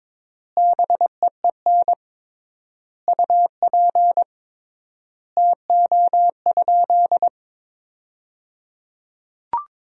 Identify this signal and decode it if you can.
Morse code, heard as a keyed tone, about 22 wpm, 700 Hz. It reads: BEEN UP TO?
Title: 3x Word-Spacing - Sentences from Top 100 Words - 22wpm